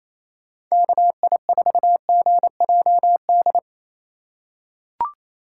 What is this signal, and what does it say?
Morse code, heard as a keyed tone, about 28 wpm, 700 Hz. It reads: KI4GJD
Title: USA Call Signs - 6 Characters - 2x Character Spacing - 28wpm